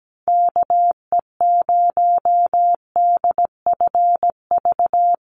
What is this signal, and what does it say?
Morse code, heard as a keyed tone, about 17 wpm, 700 Hz. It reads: KE0DFV
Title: USA Call Signs Repeated 3x 17wpm